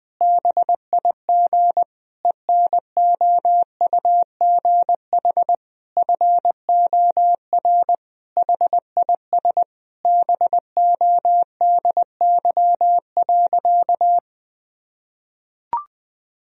Morse code, heard as a keyed tone, about 20 wpm, 700 Hz. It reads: BIG ENOUGH FOR HIS BODY.
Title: Sentences from Top 1000 Words - Review Part 2 - 20wpm